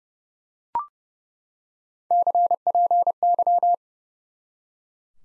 Morse code, heard as a keyed tone, about 30 wpm, 700 Hz. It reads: CPY
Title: Top 90 Abbr 30WPM